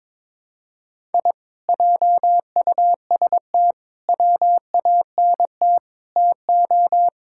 Morse code, heard as a keyed tone, about 22 wpm, 700 Hz. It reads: I JUST WANT TO
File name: practice / 22wpm / Sentences with 2-Word Phrases - 22wpm